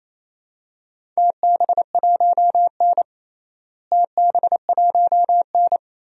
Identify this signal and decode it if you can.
Morse code, heard as a keyed tone, about 28 wpm, 700 Hz. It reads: TB1N TB1N
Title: DXCC-100 Call Signs - 4 Characters - Repeated 3x - 28wpm